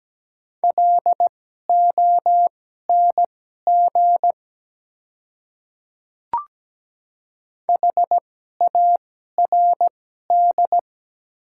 Morse code, heard as a keyed tone, about 17 wpm, spaced as for 12 wpm, 690 Hz. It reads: LONG HARD
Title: 2x Character Spacing - Top 300 Words - 17wpm